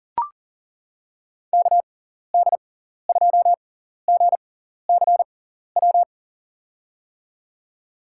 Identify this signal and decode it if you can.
Morse code, heard as a keyed tone, about 37 wpm, spaced as for 13 wpm, 700 Hz. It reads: KD2GCW
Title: USA Call Signs - 6 Characters - 6x Character Spacing - 40wpm